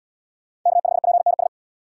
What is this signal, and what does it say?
Morse code, heard as a keyed tone, about 69 wpm, 700 Hz. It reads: K5PNS